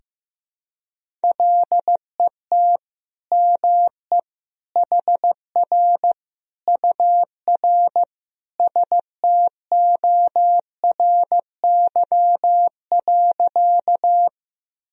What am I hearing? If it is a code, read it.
Morse code, heard as a keyed tone, about 15 wpm, 700 Hz. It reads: LET ME HR UR STORY.
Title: Sentences from Top 300 Words and Abbreviations 15wpm